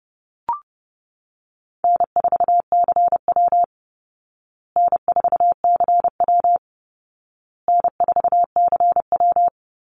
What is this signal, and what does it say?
Morse code, heard as a keyed tone, about 30 wpm, 700 Hz. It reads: N4CW N4CW N4CW
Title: Easy Call Signs Repeated 3x - 30wpm